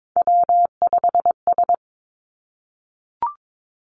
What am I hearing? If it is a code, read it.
Morse code, heard as a keyed tone, about 22 wpm, 700 Hz. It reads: W5S
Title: Three Challenging Letter-Number Combinations 22wpm